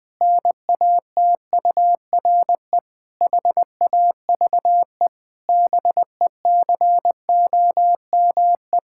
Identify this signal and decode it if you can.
Morse code, heard as a keyed tone, about 21 wpm, 700 Hz. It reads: NATURE HAVE BECOME
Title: Pangram Challenge v2 - 20wpm